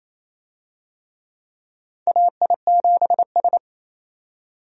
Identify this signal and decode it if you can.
Morse code, heard as a keyed tone, about 28 wpm, 700 Hz. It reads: AI7S
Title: USA Call Signs - 4 Characters - 28wpm